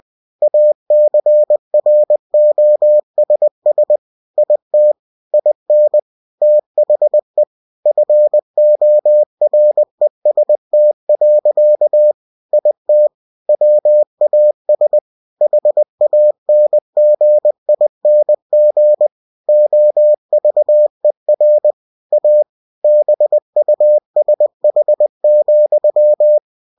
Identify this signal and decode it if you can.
Morse code, heard as a keyed tone, about 20 wpm, 600 Hz. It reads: ACROSS IT IN THE FOREST. IT WAS HANGING OVER A BUSH,
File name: cw-books / winnie-the-pooh / winnie-20wpm-page-0032